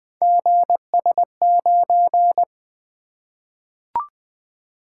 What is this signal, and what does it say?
Morse code, heard as a keyed tone, about 20 wpm, 700 Hz. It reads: GS9
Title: Three Letter-Number Combinations - 20wpm